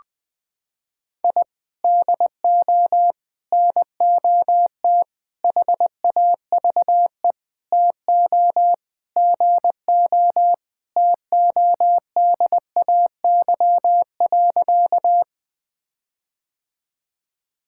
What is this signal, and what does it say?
Morse code, heard as a keyed tone, about 20 wpm, 700 Hz. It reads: I DO NOT HAVE TO GO TODAY.
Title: Sentences with 5-Word Phrases - 20wpm